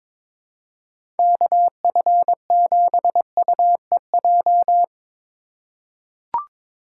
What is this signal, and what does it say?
Morse code, heard as a keyed tone, about 22 wpm, 700 Hz. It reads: KF7UEJ
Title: USA Call Signs - 6 Characters - 3x Character Spacing - 22wpm